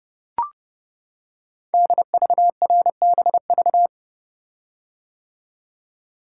Morse code, heard as a keyed tone, about 30 wpm, 700 Hz. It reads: DVRBV
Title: Code Groups - 5 Letters - 30wpm